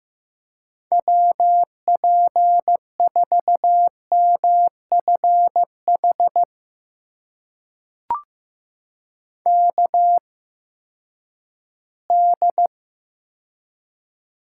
Morse code, heard as a keyed tone, about 15 wpm, spaced as for 10 wpm, 700 Hz. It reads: WP4MFH KD
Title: USA Call Signs - 8x Character Spacing - 15wpm